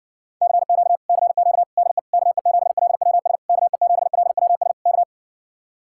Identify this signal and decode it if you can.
Morse code, heard as a keyed tone, about 60 wpm, 700 Hz. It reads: CQ CQ DE KE7LWS KE7LWS K